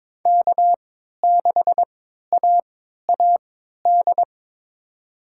Morse code, heard as a keyed tone, about 21 wpm, spaced as for 12 wpm, 700 Hz. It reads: K6AAD
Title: USA Call Signs - 5 Characters - 3x Character Spacing - 22wpm